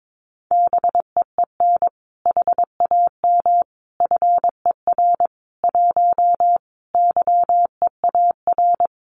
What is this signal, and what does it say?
Morse code, heard as a keyed tone, about 22 wpm, 700 Hz. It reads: BEEN HAM FER 1 YEAR
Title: Common QSO Phrases Encore 22wpm